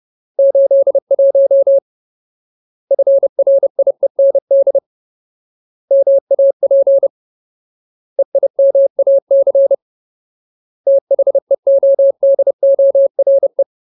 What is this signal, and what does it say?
Morse code, heard as a keyed tone, about 30 wpm, 550 Hz. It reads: 81 FRIEND MAP EIMAC THEODORE
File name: Ultimate Word List / 30WPM Ultimate